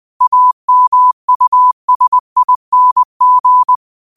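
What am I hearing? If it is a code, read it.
Morse code, heard as a keyed tone, about 20 wpm, 1000 Hz. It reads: AMUSING
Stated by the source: Morse code quote of the day at 20 wpm.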